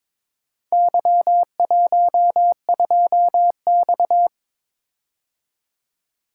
Morse code, heard as a keyed tone, about 22 wpm, 690 Hz. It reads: Y12X
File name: practice / 22wpm / Four Letter-Number Combinations - 22wpm